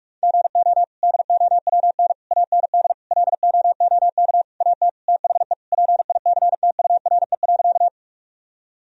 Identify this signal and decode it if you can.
Morse code, heard as a keyed tone, about 46 wpm, 700 Hz. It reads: GO DOWN AND LOOK AT THE PICTURE.
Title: Sentences from Top 1000 Words - Review Part 1 - 45wpm